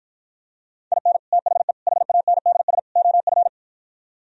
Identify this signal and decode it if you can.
Morse code, heard as a keyed tone, about 52 wpm, 700 Hz. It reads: IN THE HANDS OF